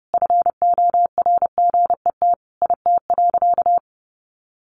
Morse code, heard as a keyed tone, about 28 wpm, 700 Hz. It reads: FORGET IT.
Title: Sentences from Top 1200 Words 30wpm